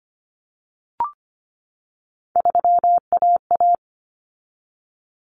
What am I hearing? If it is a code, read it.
Morse code, heard as a keyed tone, about 25 wpm, 700 Hz. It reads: 3AA
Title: Common Cut Numbers - 3-Digit 25wpm